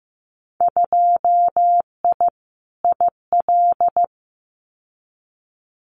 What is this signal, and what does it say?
Morse code, heard as a keyed tone, about 15 wpm, 700 Hz. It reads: 2I IL
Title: Winter Field Day Exchanges 15wpm